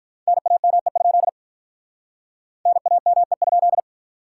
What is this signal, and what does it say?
Morse code, heard as a keyed tone, about 49 wpm, 700 Hz. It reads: NAME? NAME?